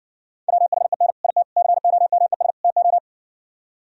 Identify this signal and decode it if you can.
Morse code, heard as a keyed tone, about 60 wpm, 700 Hz. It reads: WHEN IT COMES TO